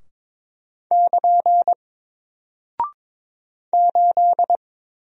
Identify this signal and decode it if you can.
Morse code, heard as a keyed tone, about 22 wpm, 700 Hz. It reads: <KN> 8
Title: Single Character (Letter:Number:Punctuation:Prosign) 22wpm